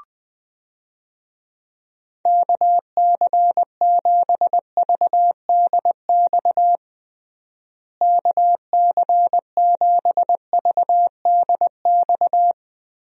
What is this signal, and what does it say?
Morse code, heard as a keyed tone, about 20 wpm, 700 Hz. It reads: KC7VDX KC7VDX
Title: USA Call Signs - 6 Characters - Repeated 3x - 20wpm